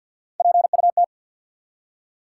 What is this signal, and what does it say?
Morse code, heard as a keyed tone, about 53 wpm, 700 Hz. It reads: PUT
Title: Three Letter Words - 50wpm